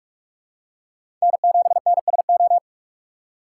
Morse code, heard as a keyed tone, about 44 wpm, 700 Hz. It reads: N7NSO
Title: USA Call Signs - 5 Characters - 3x Character Spacing - 45wpm